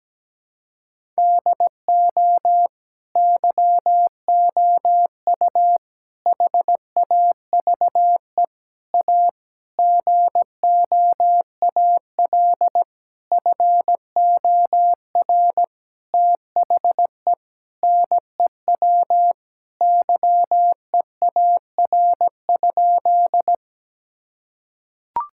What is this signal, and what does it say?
Morse code, heard as a keyed tone, about 17 wpm, 700 Hz. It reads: DO YOU HAVE A GOAL FOR THE NEW YEAR?
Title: Sentences from Top 1000 Words - Review Part 5 - 17wpm